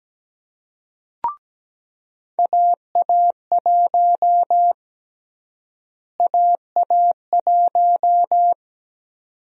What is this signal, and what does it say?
Morse code, heard as a keyed tone, about 17 wpm, 700 Hz. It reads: AA1 AA1
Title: USA Call Sign Prefixes - Repeated 2x - 17wpm